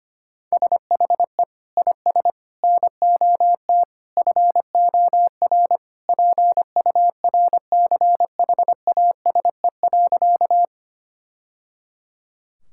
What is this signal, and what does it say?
Morse code, heard as a keyed tone, about 25 wpm, 700 Hz. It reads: SHE IS NOT FOR PURCHASE.
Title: Sentences from Top 1000 Words - Review Part 5 - 25wpm